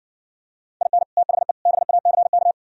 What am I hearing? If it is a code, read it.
Morse code, heard as a keyed tone, about 65 wpm, 700 Hz. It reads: IN THE BACK